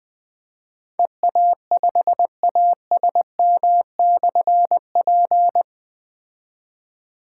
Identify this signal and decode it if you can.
Morse code, heard as a keyed tone, about 21 wpm, 700 Hz. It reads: EA5ASM/P
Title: Difficult Call Signs 20wpm